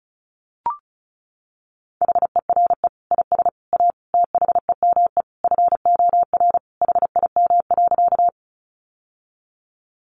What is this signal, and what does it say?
Morse code, heard as a keyed tone, about 34 wpm, 700 Hz. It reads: HERE IS A THEME FOR HIM.
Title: Sentences from Top 1100 Words 35wpm